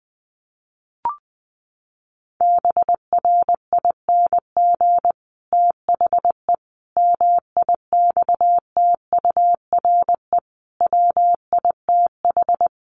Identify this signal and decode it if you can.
Morse code, heard as a keyed tone, about 20 wpm, 700 Hz. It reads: BRING THE MIXTURE WITH
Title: Sentences from Top 1500 Words 20wpm